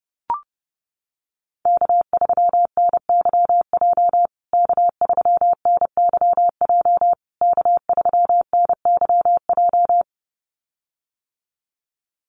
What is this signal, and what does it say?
Morse code, heard as a keyed tone, about 30 wpm, 700 Hz. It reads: K3NYJ K3NYJ K3NYJ
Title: USA Call Signs Repeated 3x 30wpm